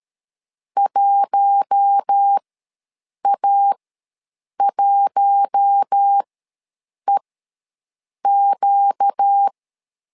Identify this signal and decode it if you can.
Morse code, heard as a keyed tone, about 13 wpm, spaced as for 7 wpm, 800 Hz. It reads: 1A1EQ
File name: LEZIONI / 4 Q1 ws 12wpm 2129